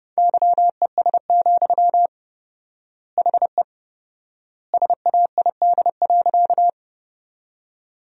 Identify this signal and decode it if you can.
Morse code, heard as a keyed tone, about 28 wpm, 700 Hz. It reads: YES, HE SAID.
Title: 4x Word-Spacing - Sentences from Top 500 Words - 30wpm